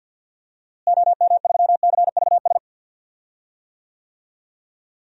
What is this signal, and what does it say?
Morse code, heard as a keyed tone, about 51 wpm, 700 Hz. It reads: OM3CVS
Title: DXCC-100 Call Signs - 6 Characters - Repeated 3x - 50wpm